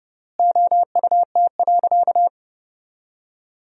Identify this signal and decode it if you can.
Morse code, heard as a keyed tone, about 30 wpm, 700 Hz. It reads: OUT.
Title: 4x Word-Spacing - Sentences from Top 500 Words - 30wpm